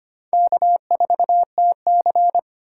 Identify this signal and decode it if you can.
Morse code, heard as a keyed tone, about 25 wpm, 700 Hz. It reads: K4TC
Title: USA Call Signs - 4 Characters - 2x Character Spacing - 25wpm